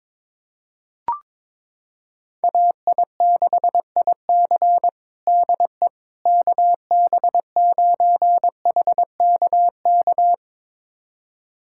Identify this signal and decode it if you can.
Morse code, heard as a keyed tone, about 22 wpm, 700 Hz. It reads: AI6IC DE KB9HKK
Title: USA Callsign DE Callsign 22wpm